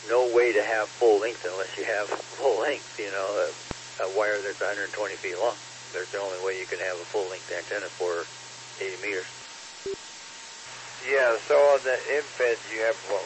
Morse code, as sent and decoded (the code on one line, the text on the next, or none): none